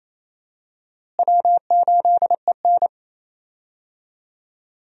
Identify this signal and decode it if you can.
Morse code, heard as a keyed tone, about 28 wpm, 700 Hz. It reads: W8EN
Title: USA Call Signs - 4 Characters - Repeated 3x - 28wpm